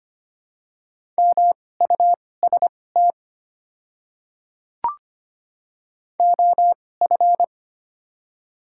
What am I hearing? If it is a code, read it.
Morse code, heard as a keyed tone, about 25 wpm, spaced as for 18 wpm, 700 Hz. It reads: MUST OF
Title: 2x Character Spacing - Top 500 Words - 25wpm